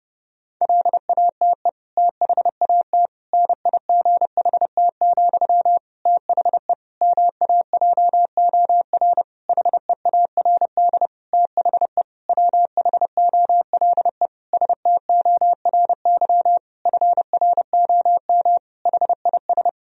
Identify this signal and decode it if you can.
Morse code, heard as a keyed tone, about 30 wpm, 700 Hz. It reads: LATE THAT NIGHT, THE MAJOR HEARD THE WHOLE STORY FROM HIS
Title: Sentences from Top 1000 Words - Review Part 4 - 30wpm